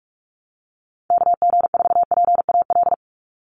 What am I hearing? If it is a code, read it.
Morse code, heard as a keyed tone, about 44 wpm, 700 Hz. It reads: KG4PAL
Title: USA Call Signs 45wpm